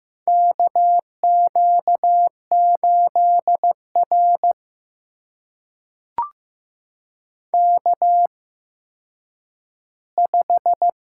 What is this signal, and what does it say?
Morse code, heard as a keyed tone, about 15 wpm, spaced as for 9 wpm, 700 Hz. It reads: KQ8R K5
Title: USA Call Signs - 4 Characters - 8x Character Spacing - 15wpm